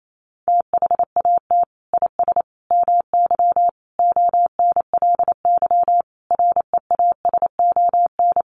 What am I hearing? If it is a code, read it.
Morse code, heard as a keyed tone, about 28 wpm, 700 Hz. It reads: THAT IS MY ONLY REASON
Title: Sentences from Top 700 Words 28wpm